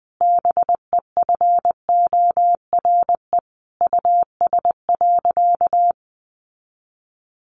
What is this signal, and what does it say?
Morse code, heard as a keyed tone, about 20 wpm, 700 Hz. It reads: BEFORE US.
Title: Sentences from Top 1400 Words 20wpm